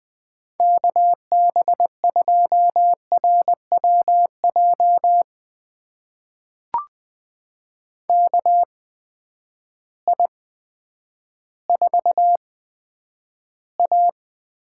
Morse code, heard as a keyed tone, about 20 wpm, spaced as for 10 wpm, 700 Hz. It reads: KB2RWJ KI4A
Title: USA Call Signs - 6 Characters - 8x Character Spacing - 20wpm